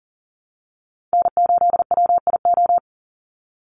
Morse code, heard as a keyed tone, about 40 wpm, 700 Hz. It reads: N8WIO
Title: USA Call Signs 40wpm